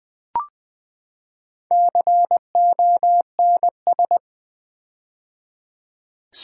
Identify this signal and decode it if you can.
Morse code, heard as a keyed tone, about 20 wpm, 700 Hz. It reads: CONS
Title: Build A Word - 4 Letters At A Time - 12-Letter Words - 20wpm